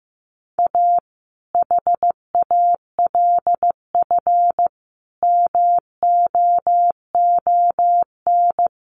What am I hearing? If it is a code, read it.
Morse code, heard as a keyed tone, about 15 wpm, 700 Hz. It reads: A HALF MOON
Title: Sentences from Top 1100 Words 15wpm